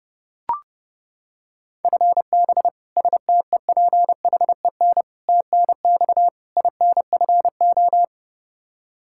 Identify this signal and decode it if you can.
Morse code, heard as a keyed tone, about 32 wpm, 700 Hz. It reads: FB STEPHEN TNX INFO
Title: Practice QSO - Exchange 3 - More Info - 30wpm